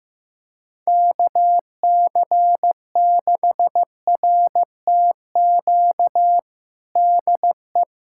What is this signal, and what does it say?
Morse code, heard as a keyed tone, about 16 wpm, 700 Hz. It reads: KC6RTQ DE
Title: USA Callsign DE Callsign - 6 Characters - 15wpm